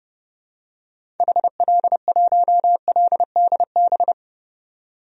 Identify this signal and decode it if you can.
Morse code, heard as a keyed tone, about 31 wpm, 700 Hz. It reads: HL1LDB
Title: DXCC-100 Call Signs - 6 Characters - 30wpm